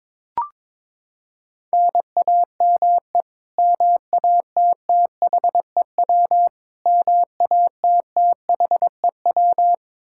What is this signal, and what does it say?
Morse code, heard as a keyed tone, about 22 wpm, 700 Hz. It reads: NAME MATTHEW MATTHEW
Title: K1USN SST QSO Practice 22wpm